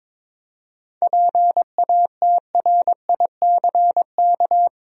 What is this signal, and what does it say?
Morse code, heard as a keyed tone, about 22 wpm, 700 Hz. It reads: PATRICK
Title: USA Common Names - 22wpm